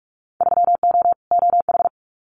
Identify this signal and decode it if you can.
Morse code, heard as a keyed tone, about 46 wpm, 700 Hz. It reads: 3O OH
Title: Winter Field Day Exchanges 45wpm